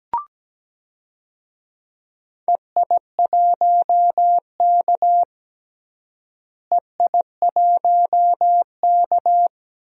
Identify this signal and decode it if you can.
Morse code, heard as a keyed tone, about 17 wpm, 700 Hz. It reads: EI1K EI1K
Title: DXCC-100 Call Signs - 4 Characters - Repeated 2x - 17wpm